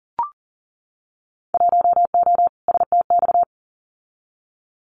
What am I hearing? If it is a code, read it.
Morse code, heard as a keyed tone, about 38 wpm, 700 Hz. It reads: 1O STX
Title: Winter Field Day Exchanges 40wpm